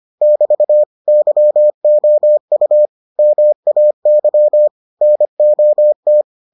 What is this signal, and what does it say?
Morse code, heard as a keyed tone, about 25 wpm, 600 Hz. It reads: <BT> YOU MAY NOT